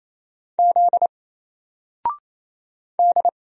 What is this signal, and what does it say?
Morse code, heard as a keyed tone, about 28 wpm, 700 Hz. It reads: Z D
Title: Single Character (Letter:Number:Punctuation:Prosign) 28wpm